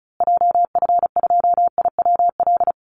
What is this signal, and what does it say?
Morse code, heard as a keyed tone, about 34 wpm, 700 Hz. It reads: JF2IWL
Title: Call Signs Repeated 2x - 35wpm